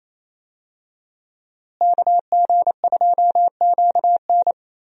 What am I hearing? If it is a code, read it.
Morse code, heard as a keyed tone, about 28 wpm, 700 Hz. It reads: KG2QN
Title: USA Call Signs - 5 Characters - Repeated 2x - 28wpm